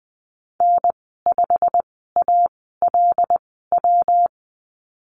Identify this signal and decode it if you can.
Morse code, heard as a keyed tone, about 20 wpm, spaced as for 15 wpm, 700 Hz. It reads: N5ALW
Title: 140 Lesson 23g - Letter f - Call Signs 20wpm